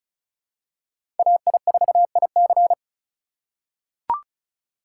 Morse code, heard as a keyed tone, about 34 wpm, 700 Hz. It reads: AI4IC
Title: USA Call Signs - 5 Characters - 2x Character Spacing - 35wpm